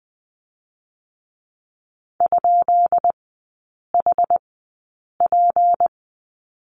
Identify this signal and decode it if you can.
Morse code, heard as a keyed tone, about 20 wpm, spaced as for 8 wpm, 700 Hz. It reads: ?HP
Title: 178 Lesson 29e - Slash - Three Characters 20wpm